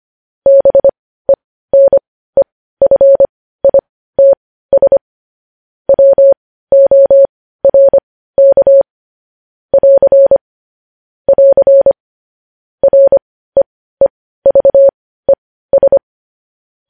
Morse code, heard as a keyed tone, about 25 wpm, spaced as for 15 wpm, 550 Hz. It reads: BENEFITS WORK <AR> <AR> REEVES
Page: news-headlines-in-morse-code-at-15-wpm